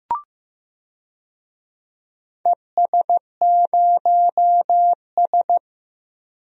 Morse code, heard as a keyed tone, about 15 wpm, 700 Hz. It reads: ES0S
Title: DXCC-100 Call Signs - 4 Characters - Repeated 3x - 15wpm